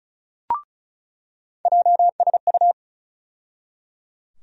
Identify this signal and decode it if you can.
Morse code, heard as a keyed tone, about 36 wpm, 700 Hz. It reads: JSU
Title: USA Call Sign Postfixes - 3 Letters 35wpm